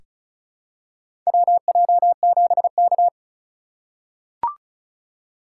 Morse code, heard as a keyed tone, about 36 wpm, 700 Hz. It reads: WJ7K
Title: USA Call Signs - 4 Characters - 4x Character Spacing - 35wpm